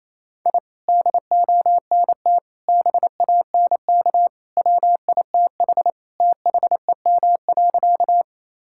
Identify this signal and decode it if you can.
Morse code, heard as a keyed tone, about 27 wpm, 700 Hz. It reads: I DONT BANK WITH THEM.
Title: Sentences from Top 1000 Words - Review Part 6 - 28wpm